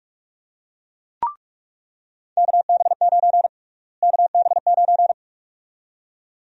Morse code, heard as a keyed tone, about 46 wpm, 700 Hz. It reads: KB9 KB9
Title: USA Call Sign Prefixes - Repeated 2x - 45wpm